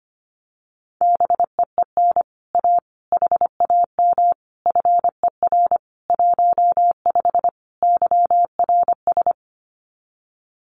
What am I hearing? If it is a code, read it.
Morse code, heard as a keyed tone, about 25 wpm, 700 Hz. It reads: BEEN A HAM FER 15 YRS
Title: Common QSO Phrases Encore 25wpm